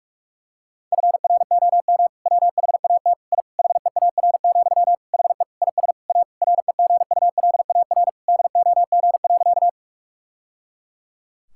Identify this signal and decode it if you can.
Morse code, heard as a keyed tone, about 41 wpm, 700 Hz. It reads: FROM WHAT I HEAR, HE IS A REGULAR DOG.
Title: Sentences from Top 1000 Words - Review Part 6 - 45wpm